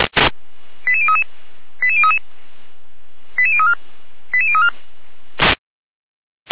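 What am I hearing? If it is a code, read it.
Five-tone selective call (ZVEI), heard as quick runs of five tones, tones of about 80 ms. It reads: 88033 88033 88034 88034
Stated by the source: ZVEI five-tone selective call, code 88102, received via SDR.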